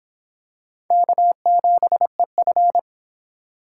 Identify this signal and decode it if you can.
Morse code, heard as a keyed tone, about 25 wpm, 700 Hz. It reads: K7EF